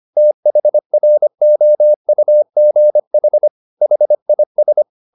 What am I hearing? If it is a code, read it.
Morse code, heard as a keyed tone, about 25 wpm, 600 Hz. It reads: THROUGH HIS